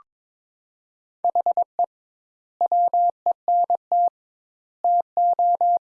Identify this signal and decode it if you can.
Morse code, heard as a keyed tone, about 22 wpm, 700 Hz. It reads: HE WENT TO
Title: Sentences Top 200 Words - 22wpm